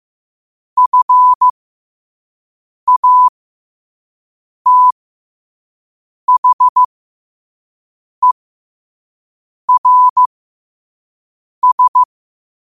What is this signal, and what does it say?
Morse code, heard as a keyed tone, about 15 wpm, spaced as for 5 wpm, 1000 Hz. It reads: FATHERS
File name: qotdfiles / 5wpm